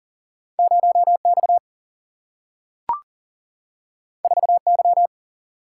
Morse code, heard as a keyed tone, about 40 wpm, 700 Hz. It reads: 0 X 4 Y
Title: Single-Letter Single-Number 40wpm